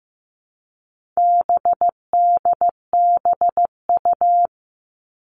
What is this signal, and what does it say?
Morse code, heard as a keyed tone, about 15 wpm, 700 Hz. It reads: BDBU
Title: All Cut Numbers - 4-Digit 15wpm